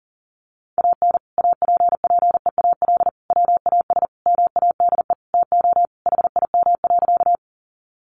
Morse code, heard as a keyed tone, about 40 wpm, 700 Hz. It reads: AN APPEAL WAS MADE TO HIM.
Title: Sentences from Top 1300 Words 40wpm